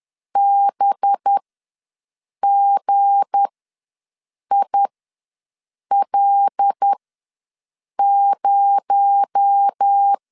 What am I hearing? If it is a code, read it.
Morse code, heard as a keyed tone, about 10 wpm, spaced as for 6 wpm, 800 Hz. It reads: BGIL0
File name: LEZIONI / 7 GF9Z / 7 G ws 10wpm 3843